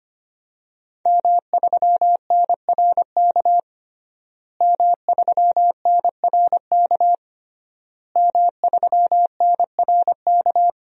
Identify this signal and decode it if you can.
Morse code, heard as a keyed tone, about 25 wpm, 700 Hz. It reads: M3NRK M3NRK M3NRK
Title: DXCC-100 Call Signs - 5 Characters - Repeated 3x - 25wpm